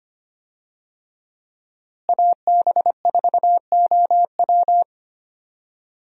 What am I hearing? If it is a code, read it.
Morse code, heard as a keyed tone, about 25 wpm, 700 Hz. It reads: AB4OW
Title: USA Call Signs - 5 Characters - Repeated 3x - 25wpm